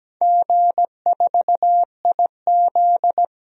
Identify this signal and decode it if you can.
Morse code, heard as a keyed tone, about 17 wpm, 700 Hz. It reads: G4IZ
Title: DXCC-100 Call Signs - 4 Characters - 17wpm